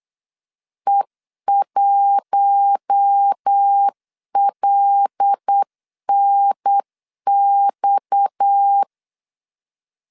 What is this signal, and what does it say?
Morse code, heard as a keyed tone, about 8 wpm, 800 Hz. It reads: E1LNX
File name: CW / DET / 1 6 1500C S C 8wpm 5219